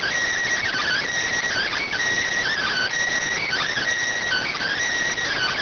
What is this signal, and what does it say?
Emergency Alert System (SAME) tones, heard as harsh buzzing bursts. It is heard from 0.6 to 4.8 s.